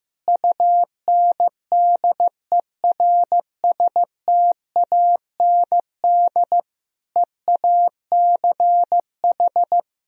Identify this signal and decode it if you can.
Morse code, heard as a keyed tone, about 15 wpm, 700 Hz. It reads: UNDERSTAND EACH
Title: Sentences from Top 1000 Words - Review Part 1 - 15wpm